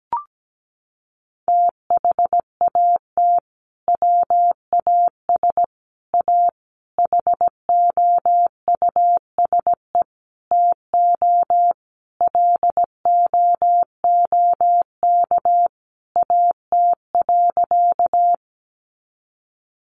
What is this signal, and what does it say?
Morse code, heard as a keyed tone, about 17 wpm, 700 Hz. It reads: THAT WAS A HOUSE TO LOOK AT.
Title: Sentences from Top 200 Words 17wpm